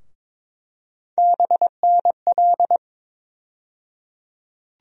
Morse code, heard as a keyed tone, about 22 wpm, 700 Hz. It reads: BNL
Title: USA Call Sign Postfixes - 3 Letters 22wpm